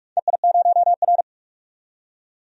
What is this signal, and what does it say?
Morse code, heard as a keyed tone, about 46 wpm, 700 Hz. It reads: EI0R
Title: DXCC-100 Call Signs - 4 Characters - 45wpm